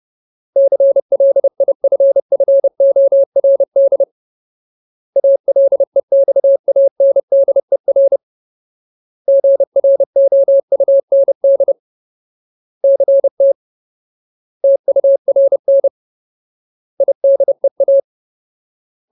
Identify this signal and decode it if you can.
Morse code, heard as a keyed tone, about 30 wpm, 550 Hz. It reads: CLIFFORD ALEXANDER GROUND CT TURN IDEA
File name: Ultimate Word List / 30WPM Ultimate